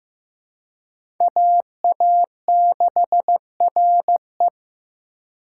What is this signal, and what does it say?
Morse code, heard as a keyed tone, about 15 wpm, 700 Hz. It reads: AA6RE
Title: K1USN SST QSO Practice - 15wpm